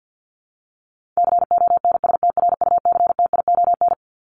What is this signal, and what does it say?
Morse code, heard as a keyed tone, about 50 wpm, 700 Hz. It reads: CONSTRUCTION
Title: Top 1501-1750 Words 50wpm